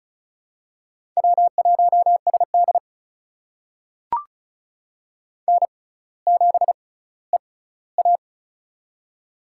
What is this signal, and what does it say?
Morse code, heard as a keyed tone, about 36 wpm, spaced as for 18 wpm, 700 Hz. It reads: W1SD N7EA
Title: USA Call Signs - 4 Characters - 6x Character Spacing - 35wpm